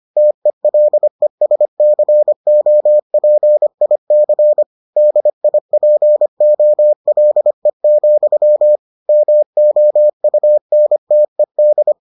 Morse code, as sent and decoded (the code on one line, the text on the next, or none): - . .-.. ... -.-. --- .--. .. -.-. / -.. .. .--. --- .-.. --..-- / -- --- ..- -. - . -..
TELESCOPIC DIPOLE, MOUNTED